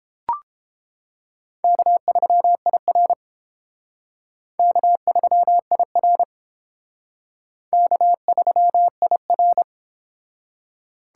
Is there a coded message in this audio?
Yes: Morse code from 1.6 to 9.6 s, heard as a keyed tone, about 28 wpm, 700 Hz. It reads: K3IR K3IR K3IR